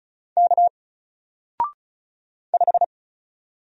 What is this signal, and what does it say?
Morse code, heard as a keyed tone, about 34 wpm, 700 Hz. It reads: K 5
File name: Single-Letter-Number-35wpm